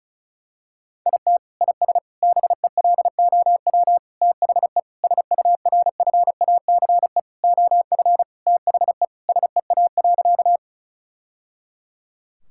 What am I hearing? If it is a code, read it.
Morse code, heard as a keyed tone, about 34 wpm, 700 Hz. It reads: IT IS BELOW THE SURFACE OF THE SEA.
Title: Sentences from Top 1000 Words - Review Part 5 - 35wpm